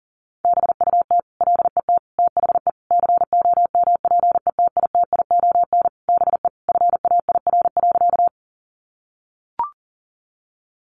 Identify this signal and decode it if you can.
Morse code, heard as a keyed tone, about 40 wpm, 700 Hz. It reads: BUT LET THE COMPETITION BE FAIR.
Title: Sentences from Top 1300 Words 40wpm